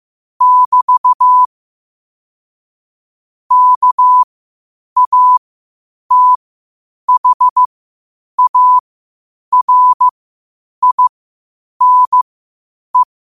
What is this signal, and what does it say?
Morse code, heard as a keyed tone, about 15 wpm, spaced as for 8 wpm, 1000 Hz. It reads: <BT> KATHARINE
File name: qotdfiles / 8wpm